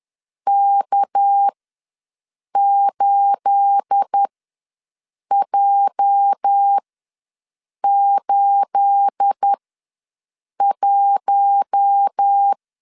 Morse code, heard as a keyed tone, about 10 wpm, spaced as for 6 wpm, 800 Hz. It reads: K8J81
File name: LEZIONI / 9 28 ws 10wpm 4124